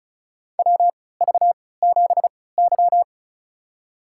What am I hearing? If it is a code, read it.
Morse code, heard as a keyed tone, about 36 wpm, spaced as for 20 wpm, 700 Hz. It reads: WV7Y
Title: USA Call Signs - 3x Character Spacing - 35wpm